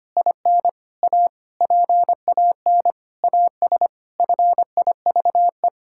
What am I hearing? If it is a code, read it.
Morse code, heard as a keyed tone, about 25 wpm, 700 Hz. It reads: IN A PAN AS FIVE
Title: Pangram Challenge v2 - 25wpm